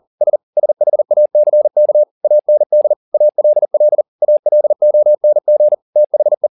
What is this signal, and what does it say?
Morse code, heard as a keyed tone, about 40 wpm, 600 Hz. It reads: S SHACK AND ALL ALONG THE